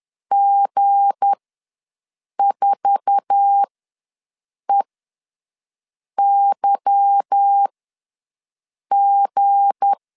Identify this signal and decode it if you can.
Morse code, heard as a keyed tone, about 10 wpm, spaced as for 5 wpm, 800 Hz. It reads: G4EYG
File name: LEZIONI / 8 YJ45 / 8 YJ4 ws 10wpm 4020